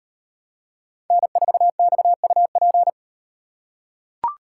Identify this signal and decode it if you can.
Morse code, heard as a keyed tone, about 38 wpm, 700 Hz. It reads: N4XUP